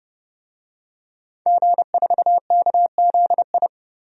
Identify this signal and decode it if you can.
Morse code, heard as a keyed tone, about 30 wpm, 700 Hz. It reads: G4KZI
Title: DXCC-100 Call Signs - 5 Characters - Repeated 2x - 30wpm